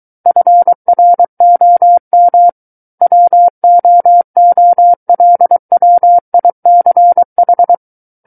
Morse code, heard as a keyed tone, about 23 wpm, 700 Hz. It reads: FROM WOOLWICH
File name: Book One / 14-Chapter 14 - 23 WPM